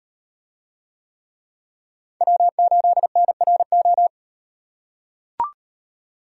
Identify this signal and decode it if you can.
Morse code, heard as a keyed tone, about 38 wpm, 700 Hz. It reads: W8NRO